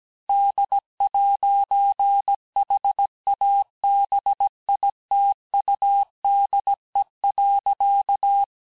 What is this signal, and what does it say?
Morse code, heard as a keyed tone, about 17 wpm, 800 Hz. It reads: D'HABITUDE.
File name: cours-du-lundi files / cours-1587-FIN-de-QSO-vitesse-16MPM